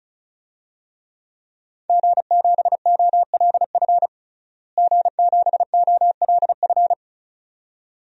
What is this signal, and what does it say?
Morse code, heard as a keyed tone, about 34 wpm, 700 Hz. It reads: G7OLF G7OLF
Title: DXCC-100 Call Signs - 5 Characters - Repeated 2x - 35wpm